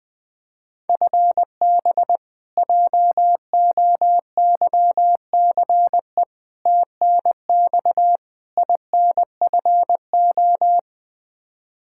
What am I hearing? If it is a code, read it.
Morse code, heard as a keyed tone, about 20 wpm, 700 Hz. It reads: FB JOYCE TNX INFO
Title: K1USN SST QSO Practice 20wpm